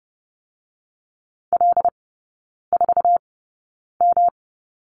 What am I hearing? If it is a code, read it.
Morse code, heard as a keyed tone, about 32 wpm, spaced as for 9 wpm, 700 Hz. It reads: L4M
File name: Lesson 18e - Letter m - Three Characters 30wpm